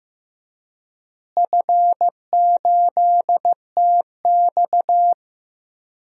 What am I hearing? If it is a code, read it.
Morse code, heard as a keyed tone, about 15 wpm, 700 Hz. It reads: F8TX
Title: DXCC-100 Call Signs - 4 Characters - Repeated 2x - 15wpm